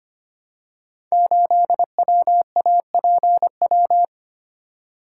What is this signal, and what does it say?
Morse code, heard as a keyed tone, about 25 wpm, 700 Hz. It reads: 8WAPW
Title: Code Groups - 5 Letter:Numbers - 25wpm